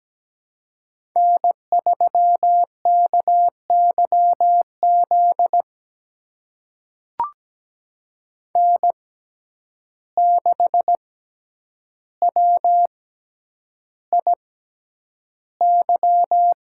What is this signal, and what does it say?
Morse code, heard as a keyed tone, about 17 wpm, spaced as for 9 wpm, 700 Hz. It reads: N3KYZ N6WIY
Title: USA Call Signs - 5 Characters - 6x Character Spacing - 17wpm